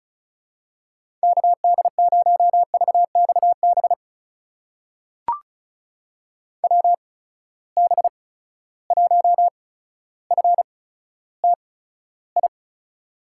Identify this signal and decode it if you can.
Morse code, heard as a keyed tone, about 36 wpm, spaced as for 15 wpm, 700 Hz. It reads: KD0VXB WB1FTI
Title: USA Call Signs - 6 Characters - 8x Character Spacing - 35wpm